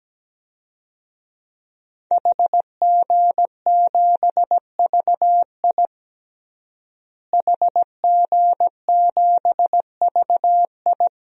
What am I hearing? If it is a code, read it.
Morse code, heard as a keyed tone, about 17 wpm, 700 Hz. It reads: HG7VI HG7VI
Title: DXCC-100 Call Signs - 5 Characters - Repeated 3x - 17wpm